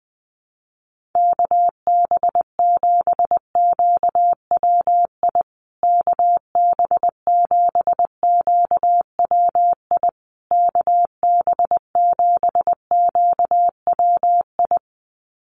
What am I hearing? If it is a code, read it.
Morse code, heard as a keyed tone, about 19 wpm, 700 Hz. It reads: KB7QWI KB7QWI KB7QWI
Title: USA Call Signs Repeated 3x 20wpm